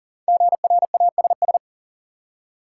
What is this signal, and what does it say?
Morse code, heard as a keyed tone, about 40 wpm, 700 Hz. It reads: GRASS
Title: Five Letter Words - 40wpm